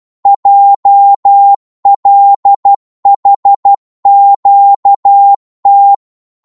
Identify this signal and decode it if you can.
Morse code, heard as a keyed tone, about 12 wpm, 800 Hz. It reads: JLHQT
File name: Morse / ltr-12wpm